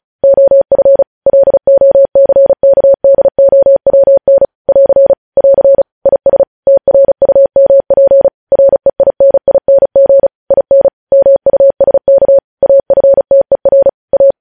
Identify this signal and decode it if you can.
Morse code, heard as a keyed tone, about 36 wpm, 550 Hz. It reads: OF LOCKDOWN <AR> <AR> IS TRUMP REINING IN MUSK AFTER A
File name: podcast / News headlines-1x-35x35